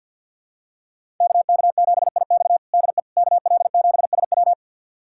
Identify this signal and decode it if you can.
Morse code, heard as a keyed tone, about 51 wpm, 700 Hz. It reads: KK7IX DE KL7SW
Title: USA Callsign DE Callsign - 5 Characters - 50wpm